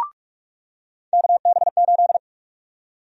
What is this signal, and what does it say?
Morse code, heard as a keyed tone, about 46 wpm, 700 Hz. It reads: KB8
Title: USA Call Sign Prefixes - 45wpm